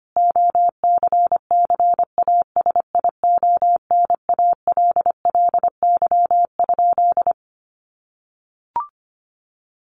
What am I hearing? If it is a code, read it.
Morse code, heard as a keyed tone, about 25 wpm, 700 Hz. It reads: OCCASIONALLY?
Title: Sentences from Top 1300 Words 25wpm